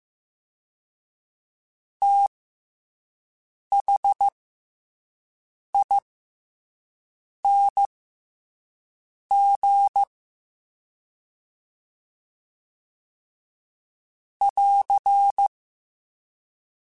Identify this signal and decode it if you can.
Morse code, heard as a keyed tone, about 15 wpm, spaced as for 5 wpm, 800 Hz. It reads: THING <AR>